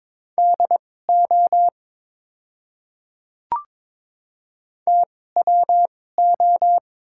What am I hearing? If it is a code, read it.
Morse code, heard as a keyed tone, about 22 wpm, spaced as for 16 wpm, 700 Hz. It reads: DO TWO
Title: 2x Character Spacing - Top 200 Words - 22wpm